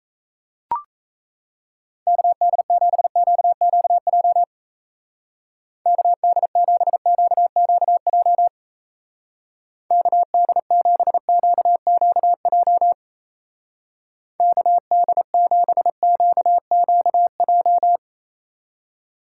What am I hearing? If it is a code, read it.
Morse code, heard as a keyed tone, about 34 wpm, 700 Hz. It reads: KD7QQJ KD7QQJ KD7QQJ KD7QQJ
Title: Speed Racing - USA Call Signs 28wpm